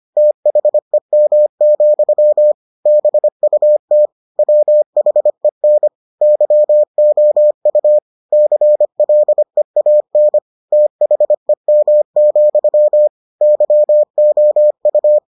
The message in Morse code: THEM, BUT WHEN YOU CLEAN THEM, YOU

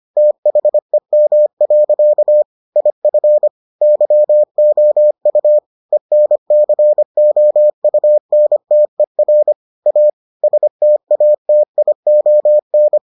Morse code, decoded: THEM. IF YOU ENCOUNTER A STATION

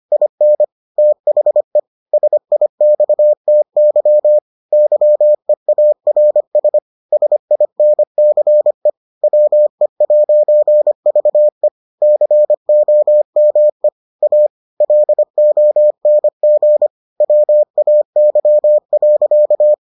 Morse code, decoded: IN THE SIXTY YEARS SINCE WE'VE COME A LONG WAY.

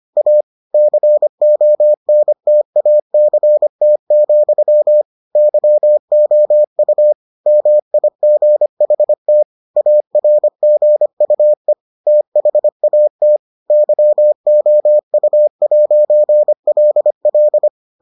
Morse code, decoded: A CONTACT, YOU MIGHT ARGUE THAT YOU'LL

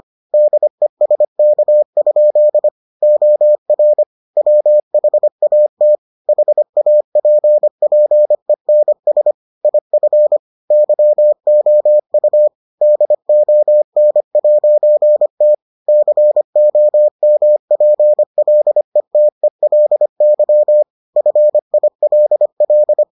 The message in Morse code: DESK? OR WHAT HAPPENS IF YOU DON'T COMPLETELY FILL

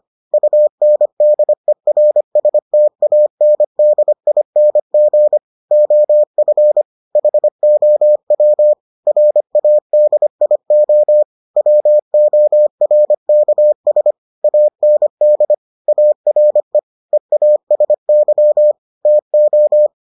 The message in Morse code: UNDERSTANDING OF HOW RADIO WORKS AND ARE EASY TO

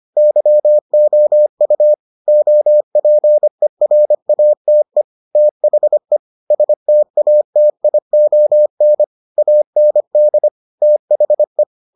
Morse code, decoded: YOU OPERATE THE STATION AND THE

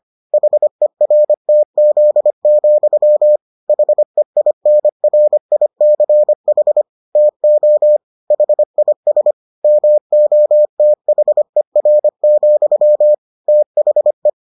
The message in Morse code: HERTZ, HEINRICH TO HIS MOTHER, THE

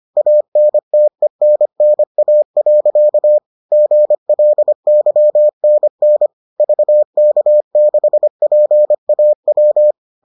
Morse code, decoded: ANTENNA. GLYNN VK6PAW

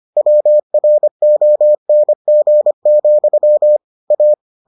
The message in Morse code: WRONG, A

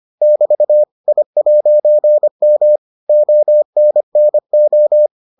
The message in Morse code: <BT> I'M ONNO

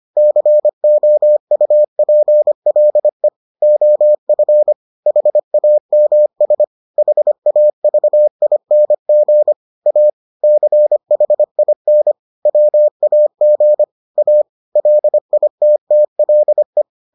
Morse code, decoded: COUPLE OF HAMS HAVING A CHIN WAG A LITTLE

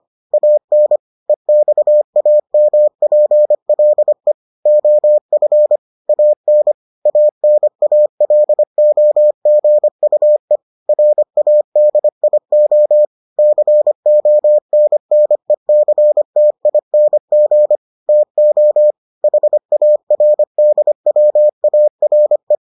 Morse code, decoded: AN EXAMPLE OF AN ANALOGUE RADIO CONNECTING TO HARDWARE

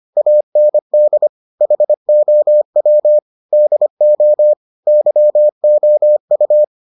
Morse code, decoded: AND HOW DO YOU

